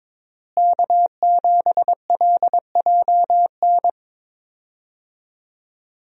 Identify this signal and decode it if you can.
Morse code, heard as a keyed tone, about 22 wpm, 700 Hz. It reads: K7LJN
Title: USA Call Signs - 5 Characters - Repeated 2x - 22wpm